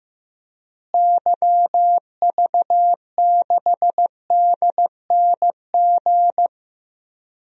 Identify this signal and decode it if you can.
Morse code, heard as a keyed tone, about 15 wpm, 700 Hz. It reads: YV6DNG
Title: DXCC-100 Call Signs - 6 Characters - Repeated 2x - 15wpm